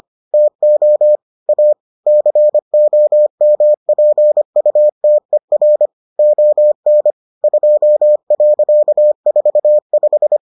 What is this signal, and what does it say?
Morse code, heard as a keyed tone, about 25 wpm, 600 Hz. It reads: TO A COMPUTER ON 2.45